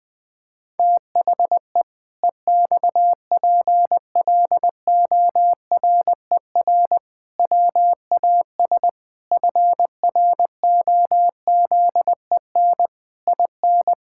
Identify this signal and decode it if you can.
Morse code, heard as a keyed tone, about 20 wpm, 700 Hz. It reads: THE EXPLORER WAS FROZEN IN
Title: Pangram Challenge v2 - 20wpm